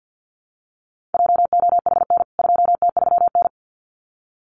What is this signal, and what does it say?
Morse code, heard as a keyed tone, about 49 wpm, 700 Hz. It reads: JOHN 2T3N
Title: CWT Exchange 50wpm